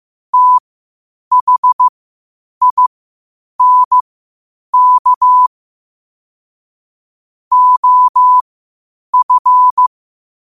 Morse code, heard as a keyed tone, about 15 wpm, spaced as for 8 wpm, 1000 Hz. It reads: THINK OF
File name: qotdfiles / 8wpm